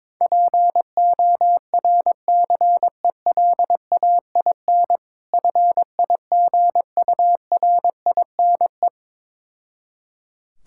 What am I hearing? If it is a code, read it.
Morse code, heard as a keyed tone, about 22 wpm, 690 Hz. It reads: PORCELAIN FIGURINE
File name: Difficult Words with Hints - 22wpm